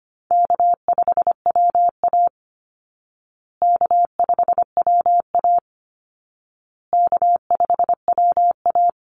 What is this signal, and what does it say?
Morse code, heard as a keyed tone, about 25 wpm, 700 Hz. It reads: K5WA K5WA K5WA
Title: Easy Call Signs Repeated 3x - 25wpm